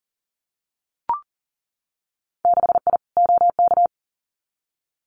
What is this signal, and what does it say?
Morse code, heard as a keyed tone, about 42 wpm, 700 Hz. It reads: 6I OK
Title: Winter Field Day Exchanges Encore 40wpm